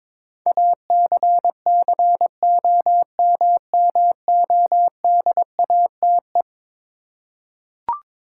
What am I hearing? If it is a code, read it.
Morse code, heard as a keyed tone, about 22 wpm, 700 Hz. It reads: ACCOMMODATE
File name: practice / 22wpm / Eleven Letter Words - 22wpm